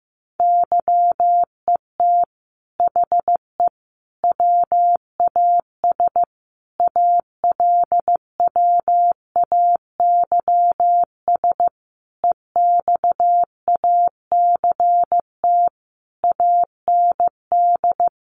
Morse code, heard as a keyed tone, about 15 wpm, 700 Hz. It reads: YET HE WAS ALWAYS EXACT AND
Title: Sentences from Top 1300 Words 15wpm